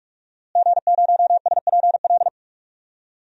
Morse code, heard as a keyed tone, about 46 wpm, 700 Hz. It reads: G0SPL
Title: DXCC-100 Call Signs - 5 Characters - Repeated 3x - 45wpm